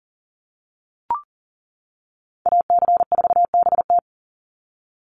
Morse code, heard as a keyed tone, about 40 wpm, 700 Hz. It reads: AC4BT
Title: Call Signs - 40wpm